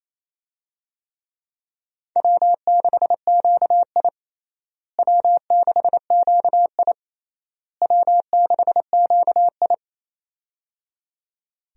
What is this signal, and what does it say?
Morse code, heard as a keyed tone, about 28 wpm, 700 Hz. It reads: W6QI W6QI W6QI
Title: USA Call Signs - 4 Characters - Repeated 3x - 28wpm